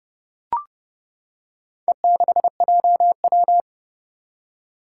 Morse code, heard as a keyed tone, about 30 wpm, 700 Hz. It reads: E6JW
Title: Four Challenging Letter-Number Combinations 30wpm